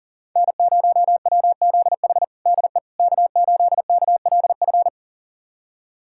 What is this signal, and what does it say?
Morse code, heard as a keyed tone, about 40 wpm, 700 Hz. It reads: N0WZH DE K8KLF
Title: USA Callsign DE Callsign - 5 Characters - 40wpm